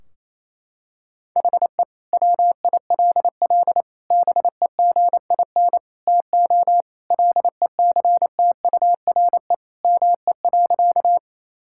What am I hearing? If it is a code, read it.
Morse code, heard as a keyed tone, about 28 wpm, 700 Hz. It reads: HE WILL BEGIN TO LECTURE ME.
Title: Sentences from Top 1600 Words 28wpm